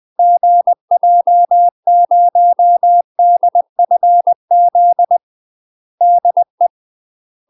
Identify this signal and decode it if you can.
Morse code, heard as a keyed tone, about 20 wpm, 700 Hz. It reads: GJ0DFZ DE
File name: UK MASTER QSOs TRAINING / 20WPM / UK MASTER QSOs 0005